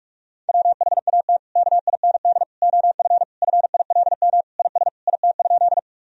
Morse code, heard as a keyed tone, about 41 wpm, 700 Hz. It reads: WHAT KIND OF FILM IS IT?
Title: Sentences from Top 1000 Words - Review Part 3 - 45wpm